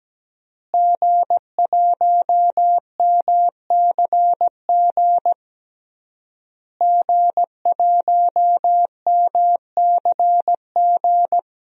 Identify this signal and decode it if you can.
Morse code, heard as a keyed tone, about 17 wpm, 700 Hz. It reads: G1MCG G1MCG
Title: DXCC-100 Call Signs - 5 Characters - Repeated 3x - 17wpm